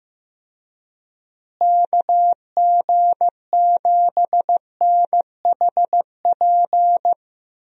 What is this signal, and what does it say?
Morse code, heard as a keyed tone, about 16 wpm, 700 Hz. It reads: KG7NHP
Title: USA Call Signs - 6 Characters - Repeated 3x - 15wpm